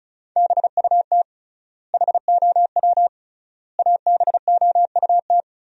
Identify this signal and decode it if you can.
Morse code, heard as a keyed tone, about 34 wpm, 700 Hz. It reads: BUT HOW ABOUT
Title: 3x Word-Spacing - Sentences from Top 100 Words - 35wpm